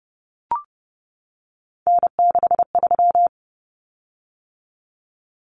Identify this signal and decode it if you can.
Morse code, heard as a keyed tone, about 30 wpm, 700 Hz. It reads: N63
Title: Common Cut Numbers - 3-Digit 30wpm